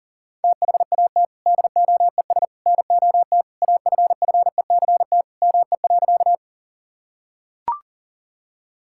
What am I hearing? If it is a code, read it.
Morse code, heard as a keyed tone, about 38 wpm, 700 Hz. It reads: THAT DOES NOT AFFECT ME.
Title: Sentences from Top 1000 Words - Review Part 6 - 40wpm